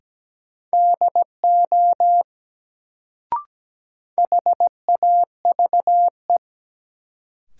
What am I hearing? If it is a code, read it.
Morse code, heard as a keyed tone, about 17 wpm, 700 Hz. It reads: DO HAVE
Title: Sentences from Top 100 Words - New Word Drill - 17wpm